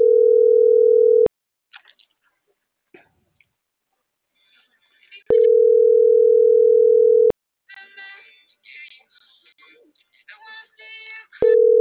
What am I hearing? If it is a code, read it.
Ringback tone (North America), 440 + 480 Hz.